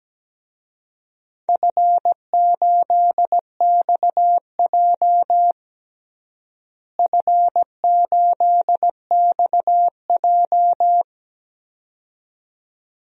Morse code, heard as a keyed tone, about 17 wpm, 700 Hz. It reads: F8XJ F8XJ
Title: DXCC-100 Call Signs - 4 Characters - Repeated 2x - 17wpm